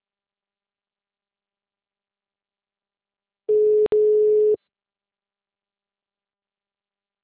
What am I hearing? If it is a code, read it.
Ringback tone (Europe, 425 Hz).
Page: Arapongas - PR